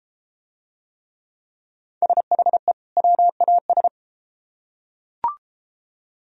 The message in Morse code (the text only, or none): SHE WAS